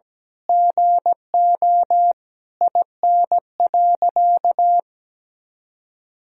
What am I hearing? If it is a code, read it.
Morse code, heard as a keyed tone, about 17 wpm, 700 Hz. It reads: GO IN.
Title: Sentences from Top 1000 Words - Review Part 4 - 17wpm